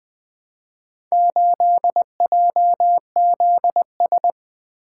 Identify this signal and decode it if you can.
Morse code, heard as a keyed tone, about 20 wpm, 700 Hz. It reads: 8JZS
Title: Four Letter-Number Combinations - 20wpm